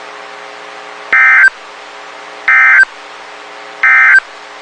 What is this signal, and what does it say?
Emergency Alert System (SAME) tones, heard as harsh buzzing bursts. It is heard from 1.0 to 4.3 s.
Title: WXK91 NOAA Weather Radio